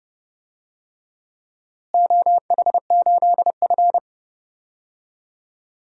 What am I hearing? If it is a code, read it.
Morse code, heard as a keyed tone, about 30 wpm, 700 Hz. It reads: OH8F
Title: DXCC-100 Call Signs - 4 Characters - Repeated 3x - 30wpm